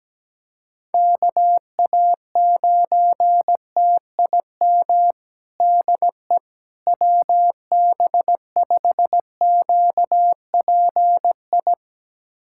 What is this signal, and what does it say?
Morse code, heard as a keyed tone, about 17 wpm, 700 Hz. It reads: KA9TIM DE WB5QPI
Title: USA Callsign DE Callsign - 6 Characters - 17wpm